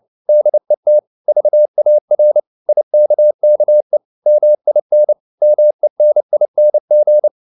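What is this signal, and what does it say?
Morse code, heard as a keyed tone, about 29 wpm, 600 Hz. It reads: DET VAR IKKE MIN MENING
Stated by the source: Fast (QRQ) Morse at 29 wpm, from CW med Gnister.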